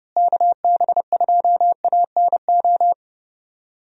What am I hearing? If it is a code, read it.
Morse code, heard as a keyed tone, about 32 wpm, 700 Hz. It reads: KB2ANO
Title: USA Call Signs - 6 Characters - Repeated 3x - 30wpm